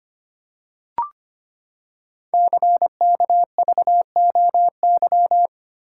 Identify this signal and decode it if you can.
Morse code, heard as a keyed tone, about 25 wpm, 700 Hz. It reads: CKVOY
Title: Code Groups - 5 Letters - 25wpm